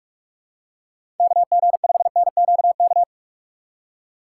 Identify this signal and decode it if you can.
Morse code, heard as a keyed tone, about 44 wpm, 700 Hz. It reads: KG5NQK
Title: USA Call Signs - 6x Character Spacing - 45wpm